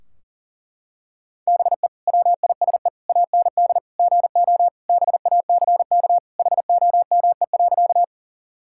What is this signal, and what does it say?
Morse code, heard as a keyed tone, about 40 wpm, 700 Hz. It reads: BE WISE AND GO BACK HOME.
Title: Sentences from Top 1600 Words 40wpm